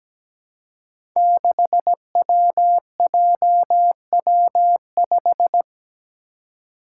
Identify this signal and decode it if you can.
Morse code, heard as a keyed tone, about 17 wpm, 700 Hz. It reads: 6WJW5
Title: Five Challenging Letter-Number Combinations 17wpm